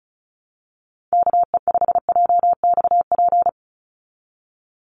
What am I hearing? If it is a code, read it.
Morse code, heard as a keyed tone, about 34 wpm, 700 Hz. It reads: KE5JXP
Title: USA Call Signs 35wpm